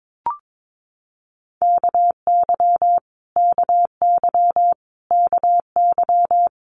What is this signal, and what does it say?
Morse code, heard as a keyed tone, about 22 wpm, 700 Hz. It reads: KY KY KY
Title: Multi-letter Phonemes Repeated 3x 22wpm